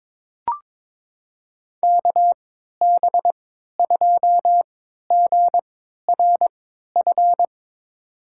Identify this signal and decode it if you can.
Morse code, heard as a keyed tone, about 22 wpm, spaced as for 13 wpm, 700 Hz. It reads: KB2GRF
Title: USA Call Signs - 6 Characters - 3x Character Spacing - 22wpm